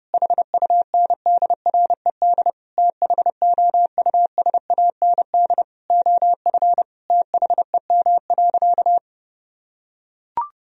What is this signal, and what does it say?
Morse code, heard as a keyed tone, about 30 wpm, 700 Hz. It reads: HUNDRED THOUSAND OF THEM.
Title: Sentences from Top 1000 Words - Review Part 3 - 30wpm